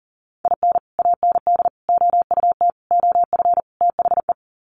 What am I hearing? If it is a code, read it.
Morse code, heard as a keyed tone, about 40 wpm, 700 Hz. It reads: IN AND OUT OF THE
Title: Sets of 5 Words 40wpm